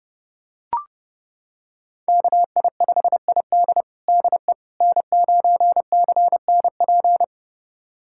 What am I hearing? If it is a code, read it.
Morse code, heard as a keyed tone, about 30 wpm, 700 Hz. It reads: KI5ID DE N9CNP
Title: USA Callsign DE Callsign - 5 Characters - 30wpm